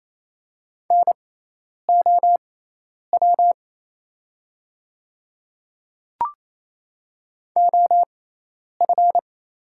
Morse code, heard as a keyed tone, about 28 wpm, spaced as for 10 wpm, 700 Hz. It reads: NOW OF
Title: 6x Character Spacing - Top 300 Words - 28wpm